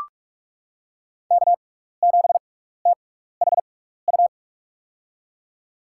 Morse code, heard as a keyed tone, about 44 wpm, spaced as for 15 wpm, 700 Hz. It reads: K7THU
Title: USA Call Signs - 5 Characters - 6x Character Spacing - 45wpm